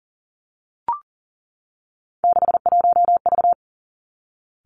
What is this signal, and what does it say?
Morse code, heard as a keyed tone, about 40 wpm, 700 Hz. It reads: 61V
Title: Three Challenging Letter-Number Combinations 40wpm